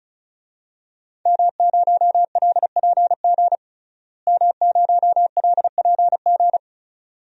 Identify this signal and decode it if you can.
Morse code, heard as a keyed tone, about 34 wpm, 700 Hz. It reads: M0LPG M0LPG
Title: DXCC-100 Call Signs - 5 Characters - Repeated 2x - 35wpm